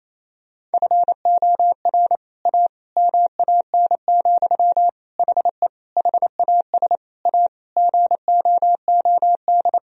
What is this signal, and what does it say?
Morse code, heard as a keyed tone, about 28 wpm, 700 Hz. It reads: FOR A MAN, HE HAS A GOOD
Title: Sentences from Top 1000 Words - Review Part 2 - 28wpm